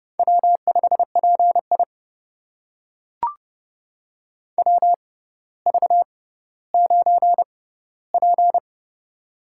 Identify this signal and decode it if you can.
Morse code, heard as a keyed tone, about 30 wpm, spaced as for 15 wpm, 700 Hz. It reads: W5PI WV9P
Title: USA Call Signs - 4 Characters - 6x Character Spacing - 30wpm